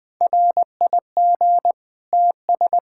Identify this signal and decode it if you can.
Morse code, heard as a keyed tone, about 20 wpm, 700 Hz. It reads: RIG TS
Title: K1USN SST QSO Practice 20wpm